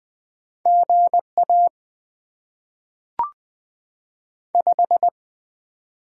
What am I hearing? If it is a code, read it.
Morse code, heard as a keyed tone, about 20 wpm, 700 Hz. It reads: GA 5
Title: Top 20 QSO Elements - 6x Character Spacing - 20wpm